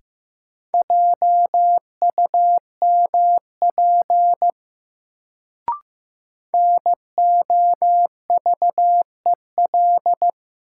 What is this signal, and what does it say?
Morse code, heard as a keyed tone, about 15 wpm, 700 Hz. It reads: JUMP NOVEL
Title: Sentences from Top 1300 Words - New Word Drill - 15wpm